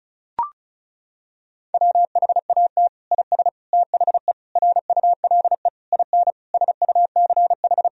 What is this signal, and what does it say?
Morse code, heard as a keyed tone, about 34 wpm, 700 Hz. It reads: WHAT IS THE RULE IN SUCH
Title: Sentences from Top 1000 Words - Review Part 2 - 35wpm